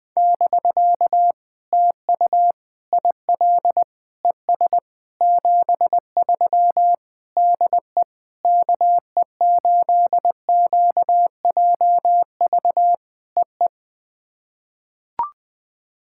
Morse code, heard as a keyed tone, about 20 wpm, 700 Hz. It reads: <BK> TU IL ES 73 DE KE8QJV EE
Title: POTA QSO Practice - 20wpm